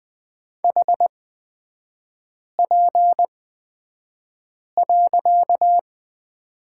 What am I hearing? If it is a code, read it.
Morse code, heard as a keyed tone, about 20 wpm, spaced as for 5 wpm, 700 Hz. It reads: HP.
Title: 255 Lesson 41e - Period - Three Characters 20wpm